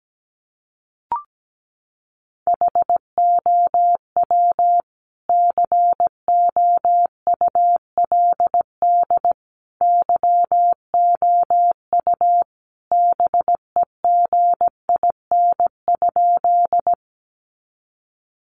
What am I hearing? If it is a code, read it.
Morse code, heard as a keyed tone, about 17 wpm, 700 Hz. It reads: HOW COULD YOU BEGIN?
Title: Sentences from Top 300 Words 17wpm